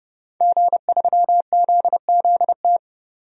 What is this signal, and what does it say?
Morse code, heard as a keyed tone, about 30 wpm, 700 Hz. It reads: G3ZZT
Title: DXCC-100 Call Signs - 5 Characters - 30wpm